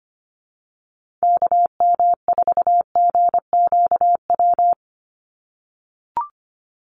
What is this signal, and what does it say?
Morse code, heard as a keyed tone, about 25 wpm, 700 Hz. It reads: KM4GQW
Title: USA Call Signs 25wpm